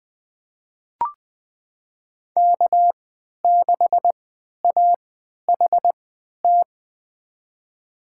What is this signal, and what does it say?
Morse code, heard as a keyed tone, about 20 wpm, spaced as for 11 wpm, 700 Hz. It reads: K6AHT
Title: USA Call Signs - 5 Characters - 3x Character Spacing - 20wpm